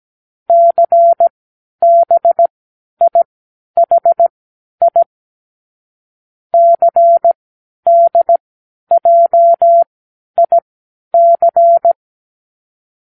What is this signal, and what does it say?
Morse code, heard as a keyed tone, about 17 wpm, spaced as for 10 wpm, 680 Hz. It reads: CBIHI CDJIC
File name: k7qo / Lesson18(A-L)